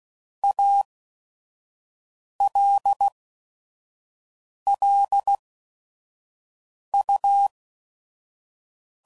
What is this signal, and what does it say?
Morse code, heard as a keyed tone, about 16 wpm, spaced as for 5 wpm, 800 Hz. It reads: ALL U
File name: CW Code Groups @5WPM / QSOs / kc7sac